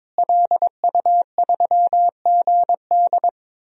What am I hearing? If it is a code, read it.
Morse code, heard as a keyed tone, about 22 wpm, 700 Hz. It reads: LU3GD
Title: DXCC-100 Call Signs - 5 Characters - Repeated 2x - 22wpm